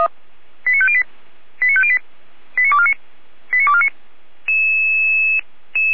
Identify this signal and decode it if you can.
Five-tone selective call (ZVEI), heard as quick runs of five tones, tones of about 70 ms. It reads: 89508 89508 89249 89249